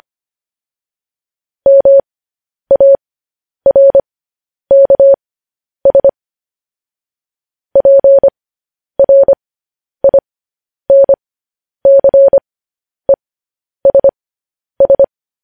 Morse code, heard as a keyed tone, about 25 wpm, spaced as for 10 wpm, 550 Hz. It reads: MARKS PRINCESS